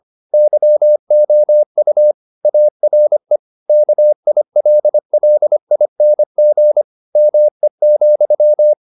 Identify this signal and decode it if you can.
Morse code, heard as a keyed tone, about 25 wpm, 600 Hz. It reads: YOU ARE KILLING ME,